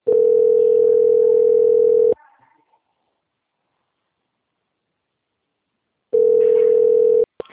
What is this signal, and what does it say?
Ringback tone (North America), 440 + 480 Hz.